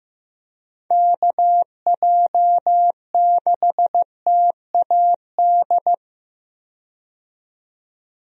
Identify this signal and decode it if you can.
Morse code, heard as a keyed tone, about 15 wpm, 700 Hz. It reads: KJ6TAD
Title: USA Call Signs - 6 Characters - 15wpm